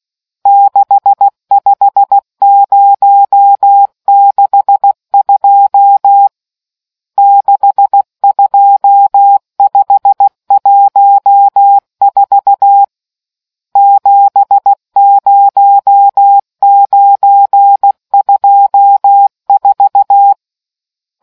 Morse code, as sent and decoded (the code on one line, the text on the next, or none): -.... ..... ----- -.... ..--- / -.... ..--- ..... .---- ....- / --... ----- ----. ..--- ....-
65062 62514 70924